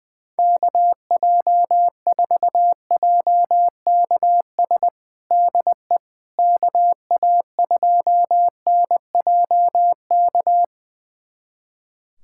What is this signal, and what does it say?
Morse code, heard as a keyed tone, about 20 wpm, 700 Hz. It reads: KJ4JKS DE KA2NJK
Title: USA Callsign DE Callsign - 6 Characters - 20wpm